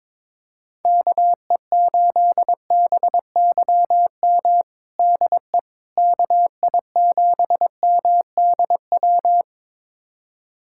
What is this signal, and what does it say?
Morse code, heard as a keyed tone, about 22 wpm, 700 Hz. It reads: KE8BYM DE KI7MDW
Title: USA Callsign DE Callsign - 6 Characters - 22wpm